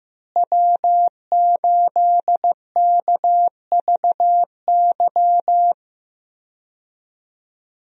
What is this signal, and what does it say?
Morse code, heard as a keyed tone, about 15 wpm, 700 Hz. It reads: W8KVY
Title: USA Call Signs - 5 Characters - Repeated 2x - 15wpm